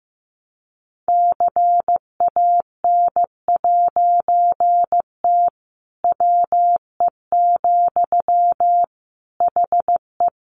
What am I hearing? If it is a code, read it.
Morse code, heard as a keyed tone, about 15 wpm, 700 Hz. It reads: CAN'T WE, HE
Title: A Sound Of Thunder 15wpm